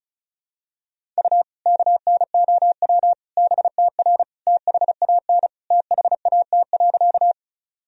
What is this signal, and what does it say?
Morse code, heard as a keyed tone, about 36 wpm, 700 Hz. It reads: U KNOW BTR THAN THAT.
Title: Sentences from Top 400 Words and Abbreviations 35wpm